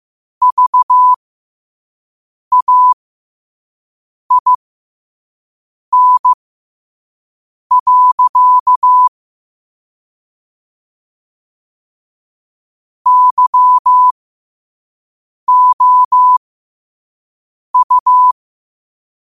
Morse code, heard as a keyed tone, about 15 wpm, spaced as for 5 wpm, 1000 Hz. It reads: VAIN. YOU
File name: qotdfiles / 5wpm